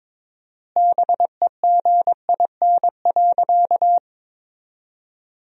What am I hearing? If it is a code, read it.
Morse code, heard as a keyed tone, about 23 wpm, 700 Hz. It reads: BEGIN.
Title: 4x Word-Spacing - Sentences from Top 300 Words - 22wpm